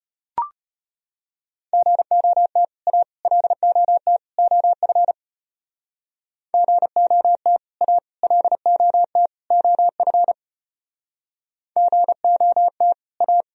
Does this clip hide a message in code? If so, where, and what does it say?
Yes: Morse code from 1.7 to 13.4 s, heard as a keyed tone, about 31 wpm, 700 Hz. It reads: GOT A LOT OF GOT A LOT OF GOT A